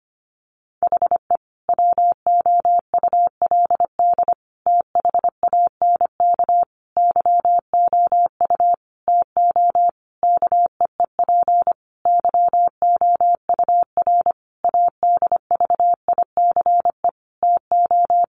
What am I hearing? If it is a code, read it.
Morse code, heard as a keyed tone, about 25 wpm, 700 Hz. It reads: HE WOULD THANK YOU TO KEEP YOUR ADVICE TO
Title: Sentences from Top 1100 Words 25wpm